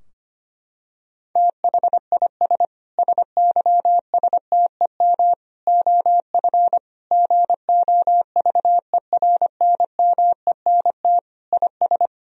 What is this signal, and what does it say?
Morse code, heard as a keyed tone, about 25 wpm, 700 Hz. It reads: THIS SYSTEM OF GOVERNMENT IS
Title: Sentences from Top 1000 Words - Review Part 3 - 25wpm